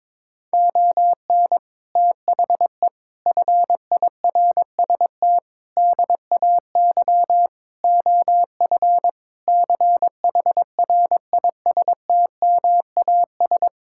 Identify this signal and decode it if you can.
Morse code, heard as a keyed tone, about 21 wpm, 700 Hz. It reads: ON THE FIRST DAY OF CHRISTMAS
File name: practice / 22wpm / Sentences with 8-Word Phrases - 22wpm